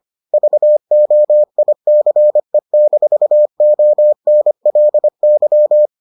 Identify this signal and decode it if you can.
Morse code, heard as a keyed tone, about 25 wpm, 600 Hz. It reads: VOICE-ONLY